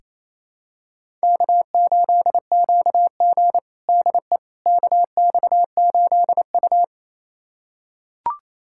Morse code, heard as a keyed tone, about 28 wpm, 700 Hz. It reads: K8QG DE KX8U
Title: USA Callsign DE Callsign - 4 Characters - 28wpm